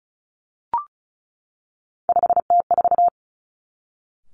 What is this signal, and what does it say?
Morse code, heard as a keyed tone, about 36 wpm, 700 Hz. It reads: 5T4
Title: Three Challenging Letter-Number Combinations 35wpm